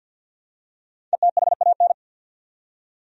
Morse code, heard as a keyed tone, about 51 wpm, 700 Hz. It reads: ETHAN
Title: USA Common Names - Max 5 Letters - 50wpm